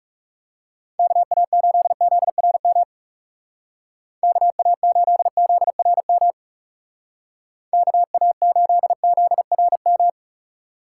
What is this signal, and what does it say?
Morse code, heard as a keyed tone, about 40 wpm, 700 Hz. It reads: KA8ZRM KA8ZRM KA8ZRM